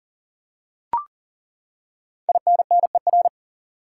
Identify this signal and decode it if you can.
Morse code, heard as a keyed tone, about 37 wpm, 700 Hz. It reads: INNER
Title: Five Letter Words - 40wpm